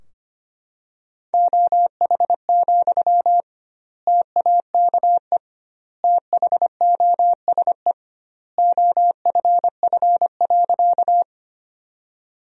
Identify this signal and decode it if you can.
Morse code, heard as a keyed tone, about 25 wpm, 700 Hz. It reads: OH, TAKE THOSE OFF.
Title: 2x Word-Spacing - Sentences from Top 500 Words - 25wpm